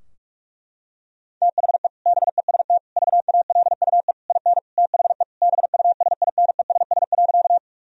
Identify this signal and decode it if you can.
Morse code, heard as a keyed tone, about 46 wpm, 700 Hz. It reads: THE BEST VALUE IN THE BUSINESS.
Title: Sentences from Top 1000 Words - Review Part 3 - 45wpm